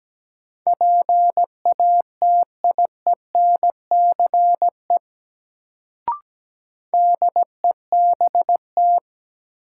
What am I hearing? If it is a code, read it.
Morse code, heard as a keyed tone, about 17 wpm, 700 Hz. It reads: PATIENCE DEBT
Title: Sentences from Top 1300 Words - New Word Drill Alt - 17wpm